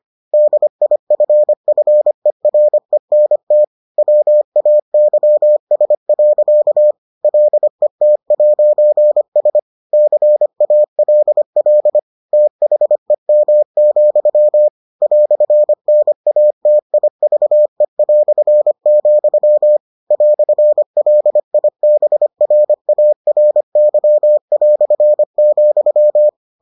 Morse code, heard as a keyed tone, about 25 wpm, 600 Hz. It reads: DIFFERENT WAYS. LET'S CALL THEM, "NATIVE", "LIBRARY",